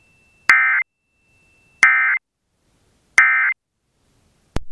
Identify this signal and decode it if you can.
Emergency Alert System (SAME) tones, heard as harsh buzzing bursts. It is heard from 0.5 to 3.5 s.